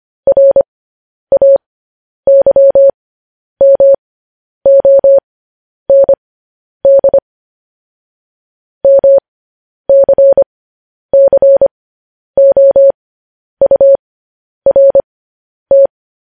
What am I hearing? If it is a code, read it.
Morse code, heard as a keyed tone, about 25 wpm, spaced as for 10 wpm, 550 Hz. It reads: RAYMOND MCCOURT
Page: news headlines in morse code at 10 wpm